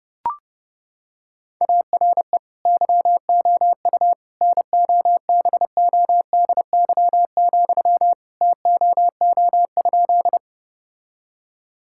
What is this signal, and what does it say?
Morse code, heard as a keyed tone, about 32 wpm, 700 Hz. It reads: ARE YOU NOBODY, TOO?
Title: Sentences from Top 1600 Words 30wpm